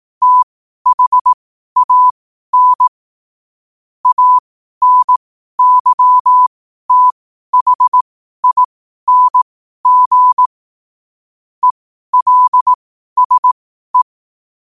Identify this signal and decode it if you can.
Morse code, heard as a keyed tone, about 18 wpm, spaced as for 12 wpm, 1000 Hz. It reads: THAN ANYTHING ELSE
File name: qotdfiles / 12wpm